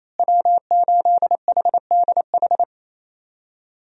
Morse code, heard as a keyed tone, about 28 wpm, 700 Hz. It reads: W8HDH
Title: USA Call Signs - 5 Characters - Repeated 2x - 28wpm